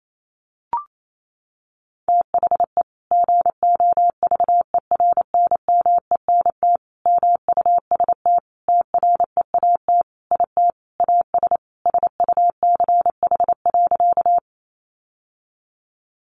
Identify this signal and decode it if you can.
Morse code, heard as a keyed tone, about 28 wpm, 700 Hz. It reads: THE GOVERNMENT MUST TREAT IT AS SUCH.
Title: Sentences from Top 1300 Words 28wpm